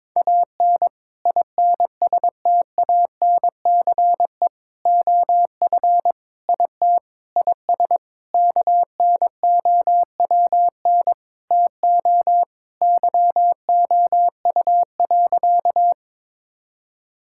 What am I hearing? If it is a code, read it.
Morse code, heard as a keyed tone, about 21 wpm, 700 Hz. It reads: AN INSTANCE OF IT IS KNOWN TO YOU.
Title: Sentences from Top 1000 Words - Review Part 5 - 22wpm